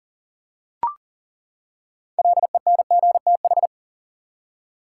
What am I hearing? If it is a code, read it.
Morse code, heard as a keyed tone, about 38 wpm, 700 Hz. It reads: LENGTH
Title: Six Letter Words - 40wpm